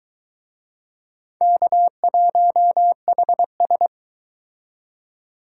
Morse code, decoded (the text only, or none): K1HS